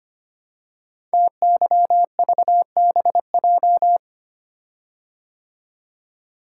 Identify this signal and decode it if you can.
Morse code, heard as a keyed tone, about 25 wpm, 700 Hz. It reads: TYVBJ
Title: Code Groups - 5 Letters - 25wpm